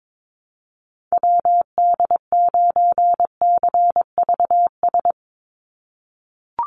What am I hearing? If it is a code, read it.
Morse code, heard as a keyed tone, about 22 wpm, 700 Hz. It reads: WD9CVS
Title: USA Call Signs 22wpm